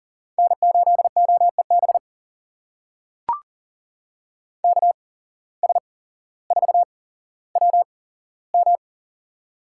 Morse code, heard as a keyed tone, about 40 wpm, spaced as for 17 wpm, 700 Hz. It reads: N8OEB KS4WM
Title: USA Call Signs - 8x Character Spacing - 40wpm